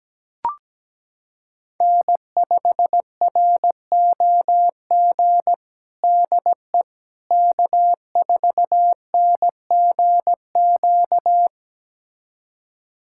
Morse code, heard as a keyed tone, about 17 wpm, 700 Hz. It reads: N5ROG DE K4NGQ
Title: USA Callsign DE Callsign - 5 Characters - 17wpm